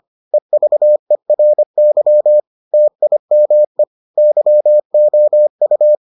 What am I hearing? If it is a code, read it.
Morse code, heard as a keyed tone, about 25 wpm, 600 Hz. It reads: EVERY TIME YOU